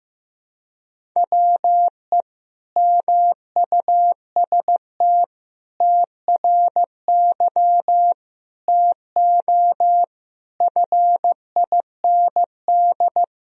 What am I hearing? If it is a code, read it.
Morse code, heard as a keyed tone, about 16 wpm, 700 Hz. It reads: WE MUST TRY TO FIND
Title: Sentences from Top 400 Words and Abbreviations 15wpm